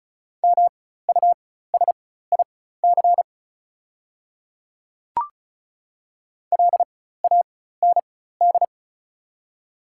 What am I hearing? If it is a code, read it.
Morse code, heard as a keyed tone, about 34 wpm, spaced as for 16 wpm, 700 Hz. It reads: MUSIC LAND
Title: 4x Character Spacing - Top 500 Words - 35wpm